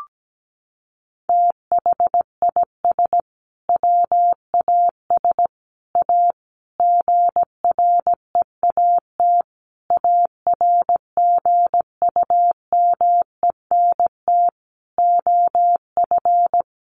Morse code, heard as a keyed tone, about 17 wpm, 700 Hz. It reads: THIS WAS A GREAT ARGUMENT OF
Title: Sentences from Top 1400 Words 17wpm